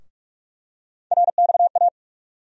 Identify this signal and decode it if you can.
Morse code, heard as a keyed tone, about 44 wpm, 700 Hz. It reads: RXA
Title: USA Call Sign Postfixes - 3 Letters 45wpm